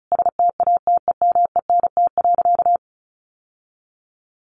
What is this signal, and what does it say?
Morse code, heard as a keyed tone, about 34 wpm, 700 Hz. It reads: STATEMENT.
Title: Sentences from Top 1200 Words 35wpm